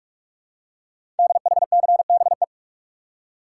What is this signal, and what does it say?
Morse code, heard as a keyed tone, about 44 wpm, 700 Hz. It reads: DHCBE
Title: Code Groups - 5 Letters - 45wpm